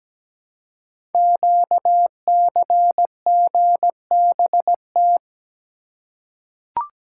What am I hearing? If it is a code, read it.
Morse code, heard as a keyed tone, about 17 wpm, 700 Hz. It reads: QCGBT
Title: Code Groups - 5 Letter:Numbers - 17wpm